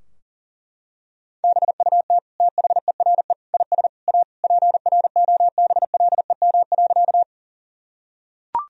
Morse code, heard as a keyed tone, about 37 wpm, 700 Hz. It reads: BUT THERE IS A PROBLEM.
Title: Sentences from Top 1000 Words - Review Part 2 - 40wpm